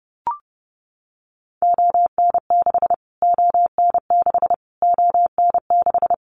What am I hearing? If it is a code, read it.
Morse code, heard as a keyed tone, about 30 wpm, 700 Hz. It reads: ON6 ON6 ON6
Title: Callsign Prefixes for Top 25 DXCC - Repeated 3x - 30wpm